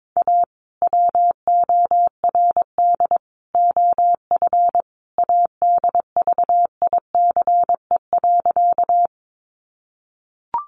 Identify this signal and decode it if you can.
Morse code, heard as a keyed tone, about 22 wpm, 700 Hz. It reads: A WORD OF ADVICE.
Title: Sentences from Top 1100 Words 22wpm